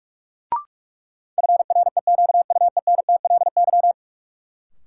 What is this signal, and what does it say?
Morse code, heard as a keyed tone, about 46 wpm, 700 Hz. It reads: FREQUENTLY
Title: Sentences from Top 1300 Words - New Word Drill Alt - 45wpm